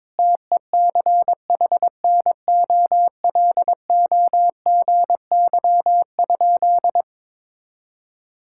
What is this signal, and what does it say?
Morse code, heard as a keyed tone, about 22 wpm, 700 Hz. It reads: TECHNOLOGY?
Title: Sentences from Top 1000 Words - Review Part 5 - 22wpm